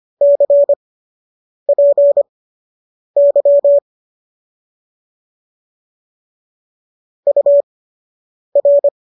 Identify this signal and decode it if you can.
Morse code, heard as a keyed tone, about 25 wpm, spaced as for 8 wpm, 570 Hz. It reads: CPY UR